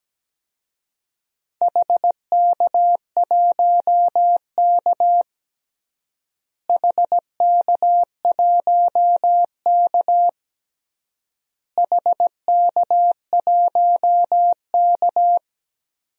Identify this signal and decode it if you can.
Morse code, heard as a keyed tone, about 17 wpm, 700 Hz. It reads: HK1K HK1K HK1K
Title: DXCC-100 Call Signs - 4 Characters - Repeated 3x - 17wpm